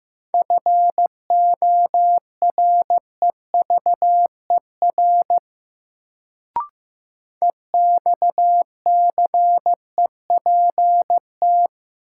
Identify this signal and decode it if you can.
Morse code, heard as a keyed tone, about 15 wpm, 700 Hz. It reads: FOREVER EXCEPT
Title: Sentences from Top 1600 Words - New Word Drill Alt - 15wpm